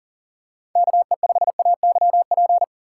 Morse code, heard as a keyed tone, about 38 wpm, 700 Hz. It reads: KE5AYP
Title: USA Call Signs - 3x Character Spacing - 40wpm